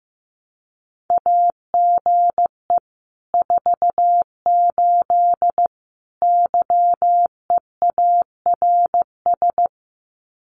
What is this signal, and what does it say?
Morse code, heard as a keyed tone, about 15 wpm, 700 Hz. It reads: AGE 48 YEARS
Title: Common QSO Phrases Encore 15wpm